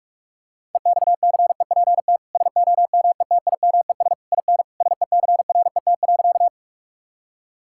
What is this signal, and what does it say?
Morse code, heard as a keyed tone, about 46 wpm, 700 Hz. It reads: EXCEPT SOMETIMES IN SECRET.
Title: Sentences from Top 1600 Words 45wpm